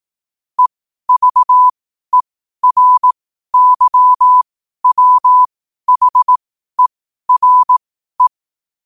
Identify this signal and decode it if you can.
Morse code, heard as a keyed tone, about 18 wpm, spaced as for 12 wpm, 1000 Hz. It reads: EVERYWHERE
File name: qotdfiles / 12wpm